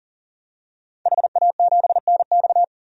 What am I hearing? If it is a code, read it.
Morse code, heard as a keyed tone, about 40 wpm, 700 Hz. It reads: HA7NX
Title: DXCC-100 Call Signs - 5 Characters - 40wpm